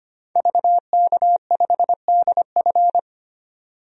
Morse code, heard as a keyed tone, about 25 wpm, 700 Hz. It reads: VK5DF
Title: DXCC-100 Call Signs - 5 Characters - 25wpm